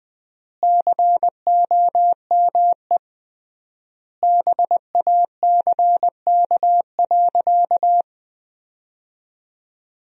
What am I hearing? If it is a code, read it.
Morse code, heard as a keyed tone, about 21 wpm, 700 Hz. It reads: COME BACK.
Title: 3x Word-Spacing - Sentences from Top 200 Words - 20wpm